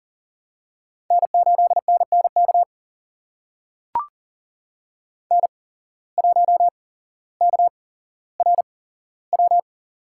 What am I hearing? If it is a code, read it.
Morse code, heard as a keyed tone, about 38 wpm, spaced as for 17 wpm, 700 Hz. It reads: N8NNK N1KRW
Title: USA Call Signs - 5 Characters - 8x Character Spacing - 40wpm